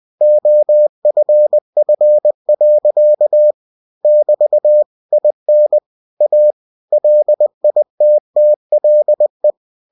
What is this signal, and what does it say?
Morse code, heard as a keyed tone, about 20 wpm, 600 Hz. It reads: OFF. <BT> IN A LITTLE